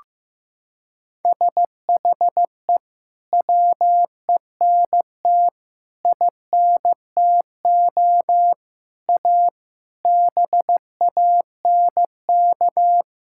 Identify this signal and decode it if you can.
Morse code, heard as a keyed tone, about 15 wpm, 700 Hz. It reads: SHE WENT INTO A BANK
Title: Sentences from Top 1000 Words - Review Part 6 - 15wpm